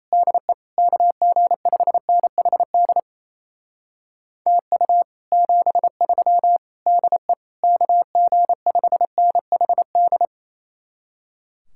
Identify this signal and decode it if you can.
Morse code, heard as a keyed tone, about 36 wpm, 700 Hz. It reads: DE KG5NHD TU 73 DE KG5NHD